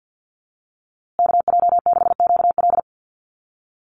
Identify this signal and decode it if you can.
Morse code, heard as a keyed tone, about 50 wpm, 700 Hz. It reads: KJ6QL
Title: USA Call Signs Repeated 3x 50wpm